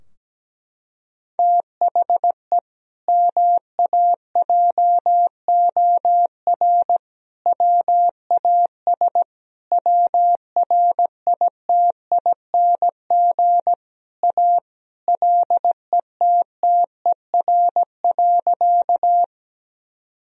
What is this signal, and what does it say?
Morse code, heard as a keyed tone, about 17 wpm, 700 Hz. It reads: THE MAJOR WAS WRITING A LETTER.
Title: Sentences from Top 1000 Words - Review Part 4 - 17wpm